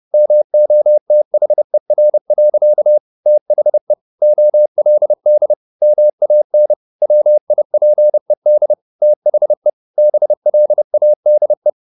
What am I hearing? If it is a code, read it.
Morse code, heard as a keyed tone, about 30 wpm, 600 Hz. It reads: MOTHER. THE OLD MAN WIPED THE BLADE